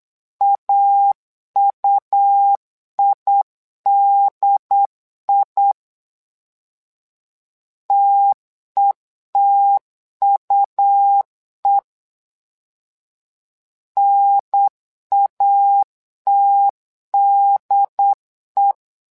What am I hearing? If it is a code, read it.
Morse code, heard as a keyed tone, about 9 wpm, 800 Hz. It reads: AUIDI TETUE NATDE
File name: LEZIONI / 2 UHDN / 2 UHDN 8wpm2232